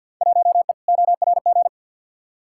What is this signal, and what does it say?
Morse code, heard as a keyed tone, about 53 wpm, 700 Hz. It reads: 1E ORG
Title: Field Day Exchanges - 50wpm